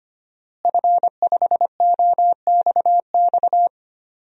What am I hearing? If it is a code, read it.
Morse code, heard as a keyed tone, about 25 wpm, 700 Hz. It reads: F5OXX
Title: DXCC-100 Call Signs - 5 Characters - Repeated 2x - 25wpm